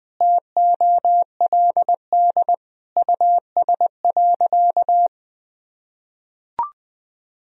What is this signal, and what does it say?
Morse code, heard as a keyed tone, about 20 wpm, 700 Hz. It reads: TOLD US.
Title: Sentences from Top 1000 Words - Review Part 3 - 20wpm